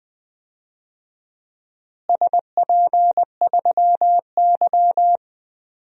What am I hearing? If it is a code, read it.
Morse code, heard as a keyed tone, about 20 wpm, 700 Hz. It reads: SP3Y
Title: DXCC-100 Call Signs - 4 Characters - Repeated 3x - 20wpm